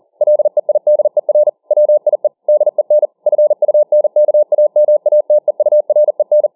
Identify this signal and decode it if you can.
Morse code, heard as a keyed tone, about 40 wpm, 600 Hz. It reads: LEIDER WIE DEN FUNKAMATEUREN